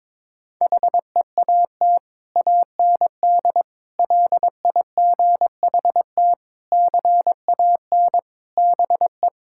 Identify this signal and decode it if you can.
Morse code, heard as a keyed tone, about 22 wpm, 700 Hz. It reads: HEAT AND LIGHT CAN BE
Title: Sentences from Top 1000 Words - Review Part 6 - 22wpm